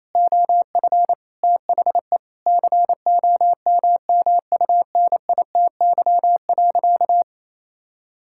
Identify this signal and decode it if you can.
Morse code, heard as a keyed tone, about 28 wpm, 700 Hz. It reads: OF THE COMMUNITY.
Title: Sentences from Top 1000 Words - Review Part 5 - 28wpm